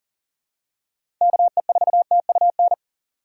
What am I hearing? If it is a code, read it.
Morse code, heard as a keyed tone, about 40 wpm, 700 Hz. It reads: KE4TUN
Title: USA Call Signs - 6 Characters - 3x Character Spacing - 40wpm